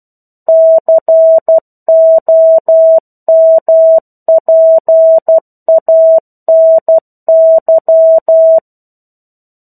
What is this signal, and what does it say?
Morse code, heard as a keyed tone, about 12 wpm, 650 Hz. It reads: COMPANY
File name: 01-Chapter-01 12 WPM